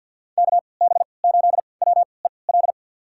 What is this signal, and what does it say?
Morse code, heard as a keyed tone, about 53 wpm, spaced as for 29 wpm, 700 Hz. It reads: KB8WEL
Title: USA Call Signs - 6 Characters - 3x Character Spacing - 50wpm